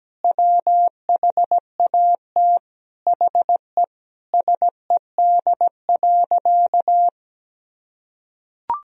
Morse code, heard as a keyed tone, about 17 wpm, 700 Hz. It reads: WHAT HE SED.
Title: Sentences from Top 400 Words and Abbreviations 17wpm